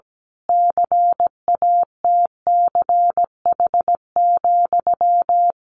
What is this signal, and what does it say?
Morse code, heard as a keyed tone, about 17 wpm, 700 Hz. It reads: CATCH,
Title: A Sound Of Thunder 17wpm